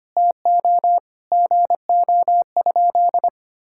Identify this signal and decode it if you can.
Morse code, heard as a keyed tone, about 25 wpm, 700 Hz. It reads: TO GO?
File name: practice / 25wpm / Sentences with 2-Word Phrases - 25wpm